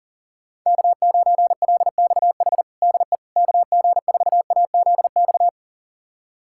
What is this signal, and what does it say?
Morse code, heard as a keyed tone, about 40 wpm, 700 Hz. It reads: K9LXH DE KG4AZX
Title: USA Callsign DE Callsign 40wpm